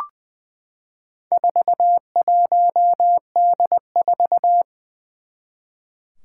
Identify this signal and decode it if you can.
Morse code, heard as a keyed tone, about 20 wpm, 700 Hz. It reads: 41D4
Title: Four Challenging Letter-Number Combinations 20wpm